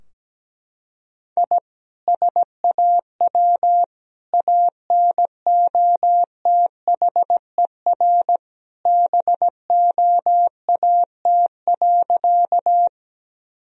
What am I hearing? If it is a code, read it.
Morse code, heard as a keyed tone, about 17 wpm, 700 Hz. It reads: I SAW ANOTHER BOAT.
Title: Sentences from Top 1000 Words - Review Part 3 - 17wpm